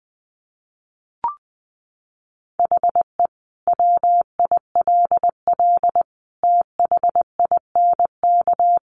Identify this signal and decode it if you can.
Morse code, heard as a keyed tone, about 20 wpm, 700 Hz. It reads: HE WILL THINK
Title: Sentences from Top 1100 Words 20wpm